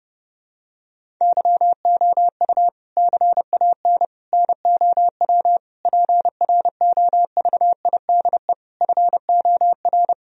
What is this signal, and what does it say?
Morse code, heard as a keyed tone, about 30 wpm, 700 Hz. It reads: YOU CAN NOW PROVIDE FOR
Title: Sentences from Top 1000 Words - Review Part 3 - 30wpm